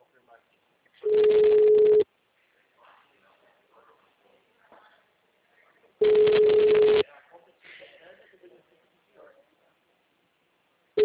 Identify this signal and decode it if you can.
Ringback tone (Europe, 425 Hz), with speech.
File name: out- -2007- - - .94